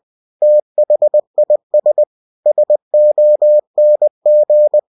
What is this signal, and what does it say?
Morse code, heard as a keyed tone, about 20 wpm, 600 Hz. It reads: THIS SONG
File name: cw-books / winnie-the-pooh / winnie-20wpm-page-0076